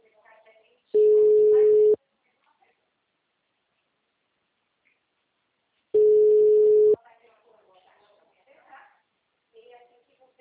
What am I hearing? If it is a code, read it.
Ringback tone (Europe, 425 Hz), with speech.
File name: out- -2044- - - .59820